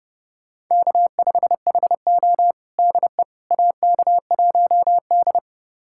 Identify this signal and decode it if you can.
Morse code, heard as a keyed tone, about 32 wpm, 700 Hz. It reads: K5HO DE AK1D
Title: USA Callsign DE Callsign - 4 Characters - 30wpm